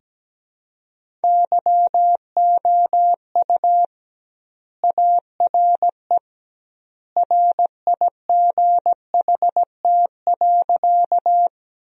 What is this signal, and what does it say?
Morse code, heard as a keyed tone, about 17 wpm, 700 Hz. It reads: YOU ARE RIGHT.
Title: 2x Word-Spacing - Sentences from Top 200 Words - 17wpm